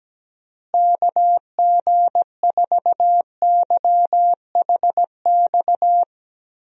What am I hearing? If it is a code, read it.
Morse code, heard as a keyed tone, about 17 wpm, 700 Hz. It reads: KG4YHX
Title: USA Call Signs - 6 Characters - Repeated 3x - 17wpm